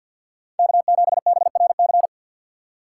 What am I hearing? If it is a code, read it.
Morse code, heard as a keyed tone, about 53 wpm, 700 Hz. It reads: K7BRC
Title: USA Call Signs - 5 Characters - 6x Character Spacing - 50wpm